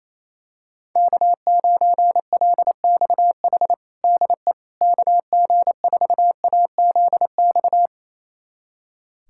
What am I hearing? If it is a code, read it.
Morse code, heard as a keyed tone, about 28 wpm, 700 Hz. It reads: K9LXH DE KG4AZX
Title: USA Callsign DE Callsign 28wpm